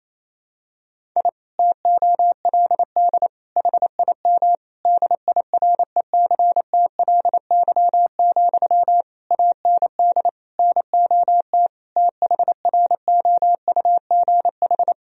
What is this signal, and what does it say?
Morse code, heard as a keyed tone, about 28 wpm, 700 Hz. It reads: I TOLD HIM DIRECTLY, AND NOT THROUGH
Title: Sentences from Top 1000 Words - Review Part 1 - 28wpm